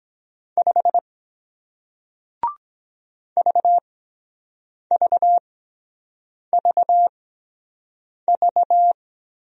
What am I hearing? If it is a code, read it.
Morse code, heard as a keyed tone, about 23 wpm, spaced as for 7 wpm, 700 Hz. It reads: 5 VVVV